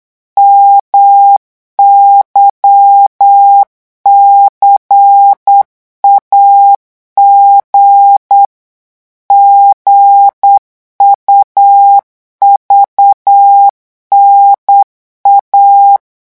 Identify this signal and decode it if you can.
Morse code, heard as a keyed tone, about 8 wpm, 800 Hz. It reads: MYCAG GUVNA
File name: morse mp3 / revision3-vitesse 8 mots